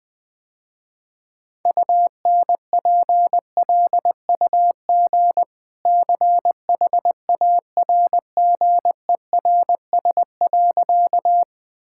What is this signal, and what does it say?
Morse code, heard as a keyed tone, about 20 wpm, 700 Hz. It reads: UNPLUG CHARGERS.
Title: Easy Sentences with Hints - 20wpm